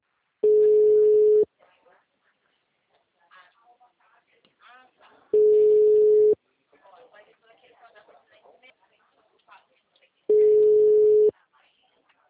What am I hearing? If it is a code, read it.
Ringback tone (Europe, 425 Hz), with speech.